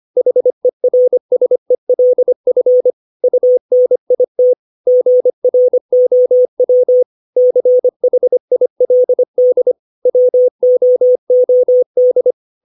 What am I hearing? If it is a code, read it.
Morse code, heard as a keyed tone, about 25 wpm, 500 Hz. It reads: HERSELF UNIT GROW CHILD WOOD
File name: 632 words / 632 Words 25WPM